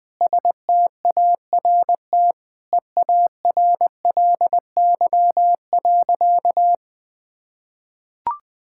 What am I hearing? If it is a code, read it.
Morse code, heard as a keyed tone, about 20 wpm, 700 Hz. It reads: START EARLY.
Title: Sentences with 9-Word Phrases - Spoken Beginning - 20wpm